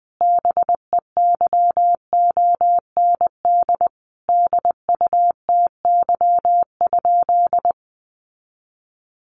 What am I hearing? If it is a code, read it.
Morse code, heard as a keyed tone, about 19 wpm, 700 Hz. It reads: BEYOND DUTY?
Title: Sentences from Top 1500 Words 20wpm